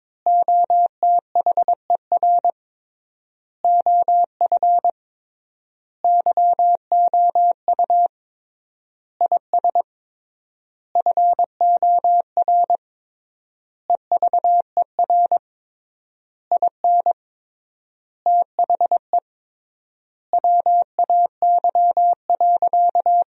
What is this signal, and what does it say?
Morse code, heard as a keyed tone, about 22 wpm, 700 Hz. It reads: OTHER OF YOU IS FOR EVER IN THE WAY.
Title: 3x Word-Spacing - Sentences from Top 400 Words - 22wpm